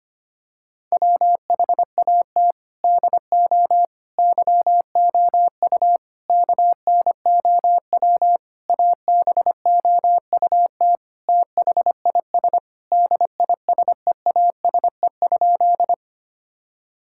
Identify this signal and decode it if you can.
Morse code, heard as a keyed tone, about 25 wpm, 700 Hz. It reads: WHAT DO YOU KNOW ABOUT THIS DISEASE?
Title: Sentences from Top 1000 Words - Review Part 6 - 25wpm